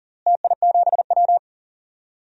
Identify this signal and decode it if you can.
Morse code, heard as a keyed tone, about 40 wpm, 700 Hz. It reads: TI7W
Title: DXCC-100 Call Signs - 4 Characters - Repeated 3x - 40wpm